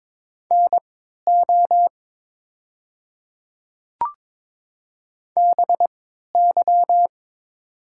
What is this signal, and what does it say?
Morse code, heard as a keyed tone, about 22 wpm, spaced as for 13 wpm, 700 Hz. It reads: NO BY
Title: 3x Character Spacing - Top 100 Words - 22wpm